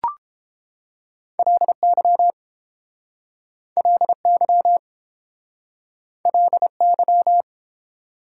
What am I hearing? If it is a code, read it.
Morse code, heard as a keyed tone, about 31 wpm, 700 Hz. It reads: LY LY LY